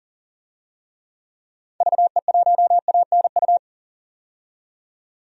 Morse code, heard as a keyed tone, about 40 wpm, 700 Hz. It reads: VE1ANU
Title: DXCC-100 Call Signs - 6 Characters - 40wpm